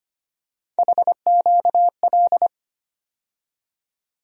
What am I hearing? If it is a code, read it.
Morse code, heard as a keyed tone, about 25 wpm, 700 Hz. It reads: HQL
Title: USA Call Sign Postfixes - 3 Letters 25wpm